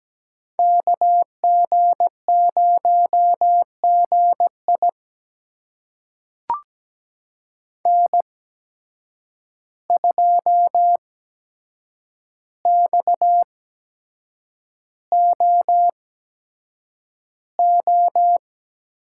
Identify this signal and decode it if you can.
Morse code, heard as a keyed tone, about 17 wpm, spaced as for 7 wpm, 700 Hz. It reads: KG0GI N2XOO
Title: USA Call Signs - 5 Characters - 8x Character Spacing - 17wpm